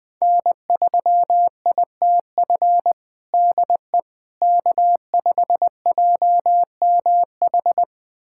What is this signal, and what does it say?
Morse code, heard as a keyed tone, about 20 wpm, 700 Hz. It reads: N3ITF DE K5JMH
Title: USA Callsign DE Callsign - 5 Characters - 20wpm